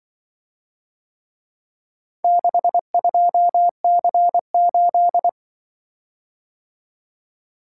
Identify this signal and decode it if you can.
Morse code, heard as a keyed tone, about 24 wpm, 700 Hz. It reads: 62C8
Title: Speed-Racing - Four Letter-Number Combinations - 20wpm